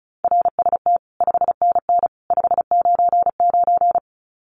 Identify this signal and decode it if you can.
Morse code, heard as a keyed tone, about 34 wpm, 700 Hz. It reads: RST 5NN 599
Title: Common QSO Phrases 35wpm